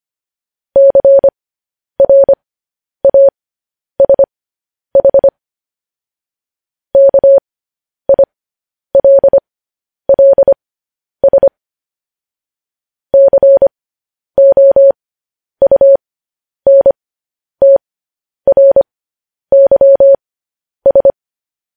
Morse code, heard as a keyed tone, about 25 wpm, spaced as for 10 wpm, 550 Hz. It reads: CRASH KILLS COUNTRYS